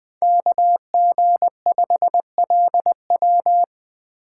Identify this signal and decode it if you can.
Morse code, heard as a keyed tone, about 20 wpm, 700 Hz. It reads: KG5LW
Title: USA Call Signs - 5 Characters - 6x Character Spacing - 20wpm